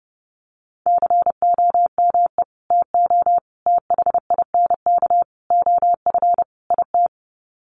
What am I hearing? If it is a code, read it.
Morse code, heard as a keyed tone, about 30 wpm, 700 Hz. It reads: COME TO THINK OF IT
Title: Sets of 5 Words Encore 30wpm